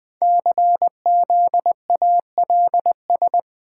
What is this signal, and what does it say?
Morse code, heard as a keyed tone, about 20 wpm, 700 Hz. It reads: CZALS
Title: Code Groups - 5 Letters - 20wpm